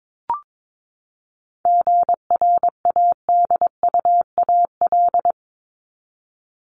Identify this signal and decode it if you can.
Morse code, heard as a keyed tone, about 22 wpm, 700 Hz. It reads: GRADUAL
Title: LY Suffix - Top 100 Words with Base Words 22wpm